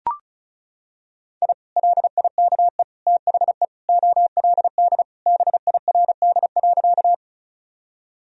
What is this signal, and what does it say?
Morse code, heard as a keyed tone, about 34 wpm, 700 Hz. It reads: I LIKE THE OLD BIRD.
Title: Sentences from Top 1000 Words - Review Part 2 - 35wpm